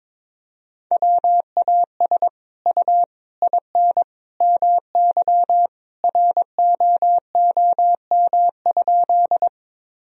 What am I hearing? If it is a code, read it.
Morse code, heard as a keyed tone, about 21 wpm, 700 Hz. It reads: WAS U IN MY ROOM?
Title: Sentences from Top 300 Words and Abbreviations 22wpm